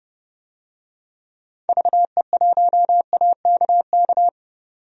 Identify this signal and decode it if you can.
Morse code, heard as a keyed tone, about 30 wpm, 700 Hz. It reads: VE1AKK
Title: DXCC-100 Call Signs - 6 Characters - 30wpm